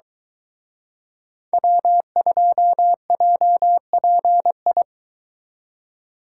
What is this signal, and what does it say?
Morse code, heard as a keyed tone, about 23 wpm, 700 Hz. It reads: W2JPI